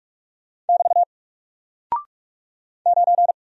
Morse code, heard as a keyed tone, about 44 wpm, 700 Hz. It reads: <BT> 9
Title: Single Character (Letter:Number:Punctuation:Prosign) 45wpm